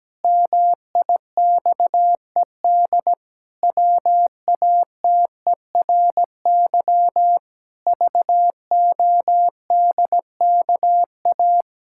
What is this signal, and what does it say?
Morse code, heard as a keyed tone, about 17 wpm, 700 Hz. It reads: MIXED WATERY VODKA
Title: Pangram Challenge v2 - 17wpm